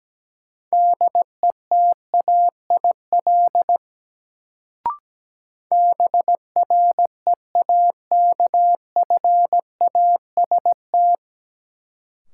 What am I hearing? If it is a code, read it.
Morse code, heard as a keyed tone, about 17 wpm, 700 Hz. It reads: DETAIL BREAKFAST
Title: Sentences from Top 1500 Words - New Word Drill - 17wpm